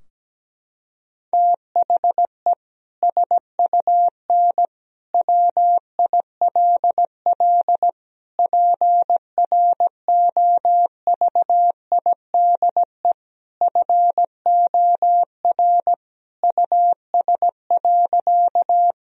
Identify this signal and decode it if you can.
Morse code, heard as a keyed tone, about 17 wpm, 700 Hz. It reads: THE SUN WILL PROVIDE FOR US.
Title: Sentences from Top 1000 Words - Review Part 3 - 17wpm